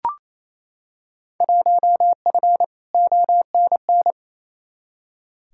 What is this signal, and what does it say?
Morse code, heard as a keyed tone, about 28 wpm, 700 Hz. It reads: 1F ONN
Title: Field Day Exchanges Encore - 28wpm